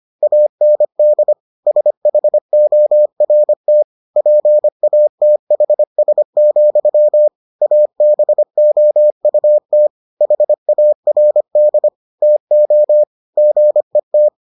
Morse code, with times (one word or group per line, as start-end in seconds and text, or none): 0.2-1.3 AND
1.7-3.8 SHORT
4.2-7.3 PATHS,
7.6-9.9 ABOUT
10.2-11.9 HARD
12.2-13.0 TO
13.4-14.3 GET